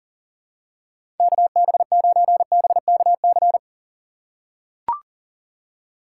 Morse code, heard as a keyed tone, about 40 wpm, 690 Hz. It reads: KB9BKC
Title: USA Call Signs - 6 Characters - 2x Character Spacing - 40wpm